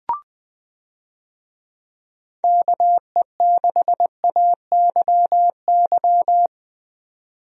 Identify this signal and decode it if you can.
Morse code, heard as a keyed tone, about 20 wpm, 700 Hz. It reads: KE6AYY
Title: USA Call Signs - 6 Characters - Repeated 2x - 20wpm